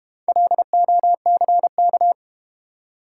Morse code, heard as a keyed tone, about 32 wpm, 700 Hz. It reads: LOCK